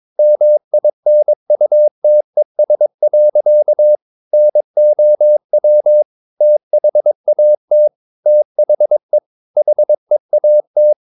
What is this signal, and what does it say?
Morse code, heard as a keyed tone, about 22 wpm, 600 Hz. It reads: MINUTES. NOW THAT THE HEAT
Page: Romanen Never af Ken Follet på engelsk.